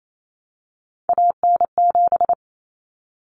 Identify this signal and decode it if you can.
Morse code, heard as a keyed tone, about 28 wpm, 700 Hz. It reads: AN7
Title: Common Cut Numbers - 3-Digit 28wpm